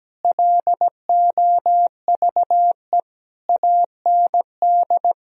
Morse code, heard as a keyed tone, about 17 wpm, 700 Hz. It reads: LOVE AND
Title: Sentences from Top 1000 Words - Review Part 3 - 17wpm